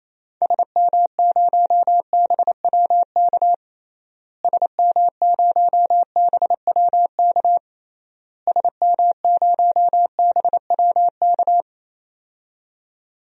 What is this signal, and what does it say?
Morse code, heard as a keyed tone, about 27 wpm, 700 Hz. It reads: SM0BWK SM0BWK SM0BWK
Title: DXCC-100 Call Signs - 6 Characters - Repeated 3x - 28wpm